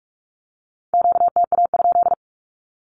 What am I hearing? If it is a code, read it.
Morse code, heard as a keyed tone, about 46 wpm, 700 Hz. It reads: QTA?
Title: Q-Codes with Spoken Meaning - 45wpm